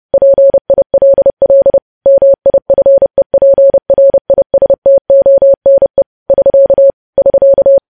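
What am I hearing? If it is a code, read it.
Morse code, heard as a keyed tone, about 30 wpm, 550 Hz. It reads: PILL MIFEPRISTONE <SK> <SK>
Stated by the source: Morse code news headlines sent at 30.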